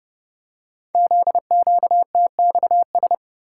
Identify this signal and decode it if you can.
Morse code, heard as a keyed tone, about 30 wpm, 700 Hz. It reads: ZQTXS
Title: Code Groups - 5 Letters - 30wpm